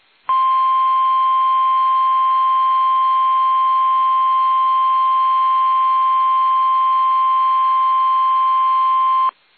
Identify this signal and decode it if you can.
Emergency Alert System (SAME) tones (the 1050 Hz attention tone). It is heard from 0.2 to 9.3 s.
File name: eas / KUSO / KUSO